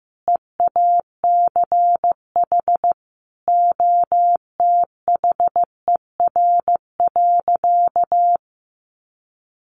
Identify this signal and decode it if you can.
Morse code, heard as a keyed tone, about 15 wpm, 700 Hz. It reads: EACH OTHER.
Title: Sentences from Top 1300 Words 15wpm